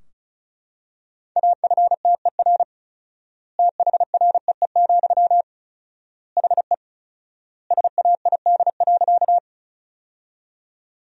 Morse code, heard as a keyed tone, about 34 wpm, 700 Hz. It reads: AFTER THREE, HE SAID.
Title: 4x Word-Spacing - Sentences from Top 200 Words - 35wpm